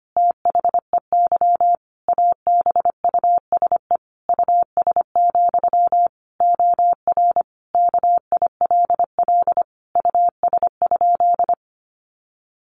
Morse code, heard as a keyed tone, about 25 wpm, 700 Hz. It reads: THEY ABUSE US, OR KILL US?
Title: Sentences from Top 1400 Words 25wpm